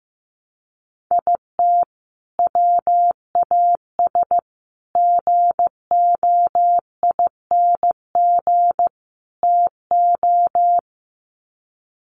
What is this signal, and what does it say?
Morse code, heard as a keyed tone, about 15 wpm, 700 Hz. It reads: IT WAS GOING TO
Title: Sets of 4 Words 15wpm